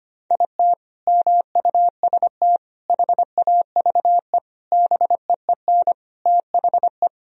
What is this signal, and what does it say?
Morse code, heard as a keyed tone, about 25 wpm, 700 Hz. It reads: IT MUST HAVE BEEN THE
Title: Sentences from Top 1000 Words - Review Part 2 - 25wpm